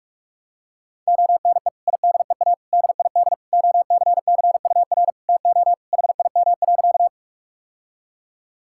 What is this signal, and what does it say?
Morse code, heard as a keyed tone, about 41 wpm, 700 Hz. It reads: ONE IDEA DID OCCUR TO HIM.
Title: Sentences from Top 1000 Words - Review Part 6 - 45wpm